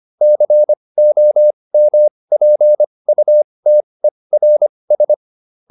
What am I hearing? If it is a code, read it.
Morse code, heard as a keyed tone, about 25 wpm, spaced as for 20 wpm, 600 Hz. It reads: COMPUTERS